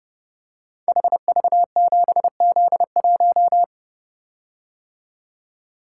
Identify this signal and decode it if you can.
Morse code, heard as a keyed tone, about 30 wpm, 700 Hz. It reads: HV7Z1
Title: Code Groups - 5 Letter:Numbers - 30wpm